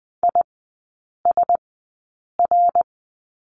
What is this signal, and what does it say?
Morse code, heard as a keyed tone, about 20 wpm, spaced as for 8 wpm, 700 Hz. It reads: ISR
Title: 074 Lesson 13e - Letter L - Three Characters 20wpm